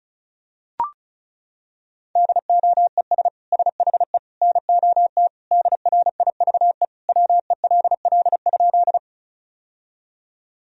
Morse code, heard as a keyed tone, about 36 wpm, 700 Hz. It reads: DOES SHE NOT DRIVE WELL?
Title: Sentences from Top 1000 Words - Review Part 2 - 35wpm